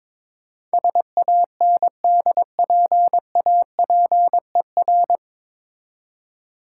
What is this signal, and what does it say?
Morse code, heard as a keyed tone, about 22 wpm, 690 Hz. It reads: SANDPAPER
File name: Difficult Words with Hints - 22wpm